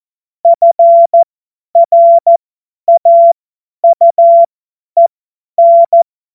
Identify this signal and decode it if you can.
Morse code, heard as a keyed tone, about 14 wpm, spaced as for 10 wpm, 680 Hz. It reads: FRAUEN
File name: Morsekurs / Witze / Witz 28. März 2023 50 F70